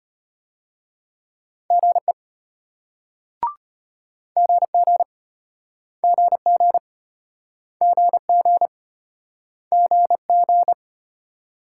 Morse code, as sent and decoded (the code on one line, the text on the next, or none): --. / --. --. / --. --. / --. --. / --. --.
GE GG GG GG GG